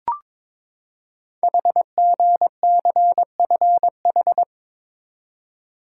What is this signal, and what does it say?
Morse code, heard as a keyed tone, about 22 wpm, 700 Hz. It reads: HGCFH
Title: Code Groups - 5 Letters - 22wpm